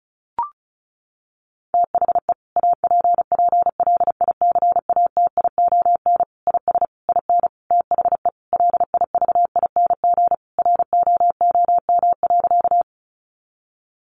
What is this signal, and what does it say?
Morse code, heard as a keyed tone, about 34 wpm, 700 Hz. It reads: THE APPLICATION IS IN THE LIVING ROOM.
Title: Sentences from Top 1200 Words 35wpm